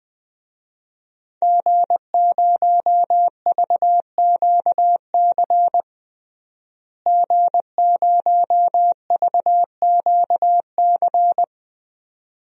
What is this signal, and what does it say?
Morse code, heard as a keyed tone, about 21 wpm, 700 Hz. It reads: G0VQC G0VQC
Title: DXCC-100 Call Signs - 5 Characters - Repeated 2x - 20wpm